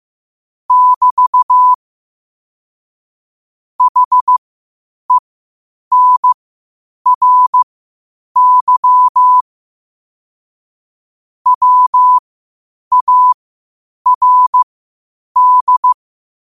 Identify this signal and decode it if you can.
Morse code, heard as a keyed tone, about 15 wpm, spaced as for 8 wpm, 1000 Hz. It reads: <BT> HENRY WARD